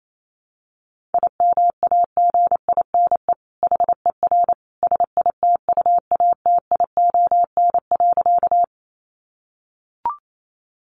Morse code, heard as a keyed tone, about 28 wpm, 700 Hz. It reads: IMAGINE HER SITUATION.
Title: Sentences from Top 1400 Words 28wpm